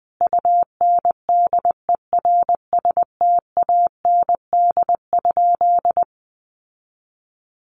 Morse code, heard as a keyed tone, about 20 wpm, 700 Hz. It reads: UNDERSTAND?
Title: Sentences from Top 600 Words 20wpm